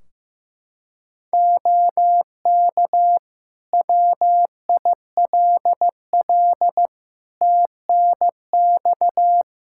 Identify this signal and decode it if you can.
Morse code, heard as a keyed tone, about 15 wpm, 700 Hz. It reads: OK WILL TNX
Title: Practice QSO - Exchange 4 - Ending - 15wpm